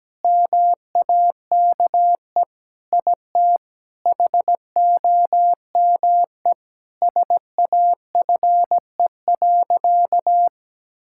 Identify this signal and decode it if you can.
Morse code, heard as a keyed tone, about 17 wpm, 700 Hz. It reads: MAKE IT HOME SAFE.
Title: Sentences from Top 1600 Words 17wpm